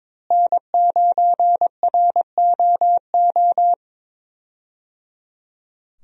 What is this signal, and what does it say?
Morse code, heard as a keyed tone, about 22 wpm, 700 Hz. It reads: N9ROO
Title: USA Call Signs - 5 Characters - Repeated 3x - 22wpm